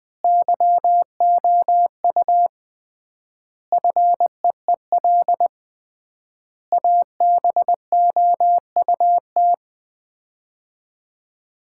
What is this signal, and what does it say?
Morse code, heard as a keyed tone, about 21 wpm, 700 Hz. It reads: YOU FEEL ABOUT
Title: 3x Word-Spacing - Sets of 5 Words - 20wpm